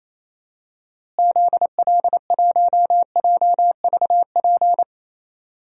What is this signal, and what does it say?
Morse code, heard as a keyed tone, about 27 wpm, 700 Hz. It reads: ZL1JVP
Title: DXCC-100 Call Signs - 6 Characters - Repeated 3x - 28wpm